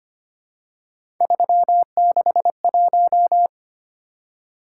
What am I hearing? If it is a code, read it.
Morse code, heard as a keyed tone, about 25 wpm, 700 Hz. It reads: 361
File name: Three Letter-Number Combinations - 25wpm